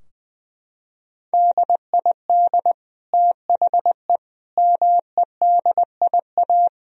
Morse code, heard as a keyed tone, about 20 wpm, 700 Hz. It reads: DID THE MEDIA
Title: Sentences from Top 1000 Words - Review Part 4 - 20wpm